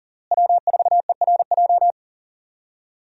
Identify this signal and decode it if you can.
Morse code, heard as a keyed tone, about 40 wpm, 700 Hz. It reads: W4ERJ
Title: USA Call Signs - 5 Characters - Repeated 3x - 40wpm